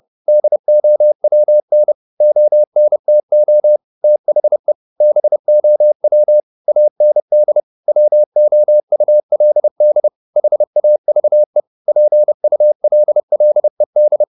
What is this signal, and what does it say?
Morse code, heard as a keyed tone, about 30 wpm, 600 Hz. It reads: DOWN ONTO THE BOW AND WOULD HAVE PULLED
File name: the old man and the sea / the old man and the sea 1 30wpm